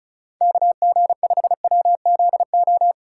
Morse code, heard as a keyed tone, about 34 wpm, 700 Hz. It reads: KG5WZO
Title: USA Callsign DE Callsign 35wpm